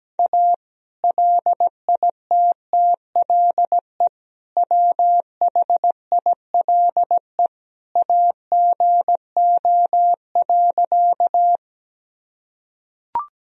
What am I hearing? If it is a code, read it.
Morse code, heard as a keyed tone, about 17 wpm, 700 Hz. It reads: A LITTLE WHILE AGO.
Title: Sentences from Top 1000 Words - Review Part 3 - 17wpm